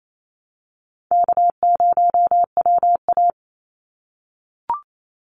Morse code, heard as a keyed tone, about 27 wpm, 700 Hz. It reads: K0WA
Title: Easy Call Signs - 28wpm